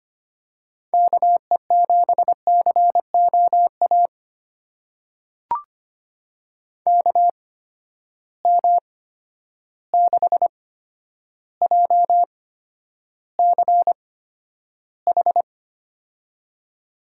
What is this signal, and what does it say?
Morse code, heard as a keyed tone, about 25 wpm, spaced as for 11 wpm, 700 Hz. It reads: KE7COA KM6JCH
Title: USA Call Signs - 6 Characters - 8x Character Spacing - 25wpm